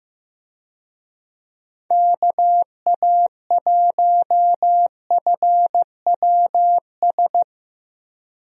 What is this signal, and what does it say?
Morse code, heard as a keyed tone, about 16 wpm, 700 Hz. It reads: KA1FWS
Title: USA Call Signs - 6 Characters - Repeated 2x - 15wpm